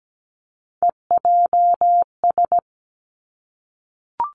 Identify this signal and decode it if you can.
Morse code, heard as a keyed tone, about 17 wpm, 700 Hz. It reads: EJS
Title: Three Challenging Letter-Number Combinations 17wpm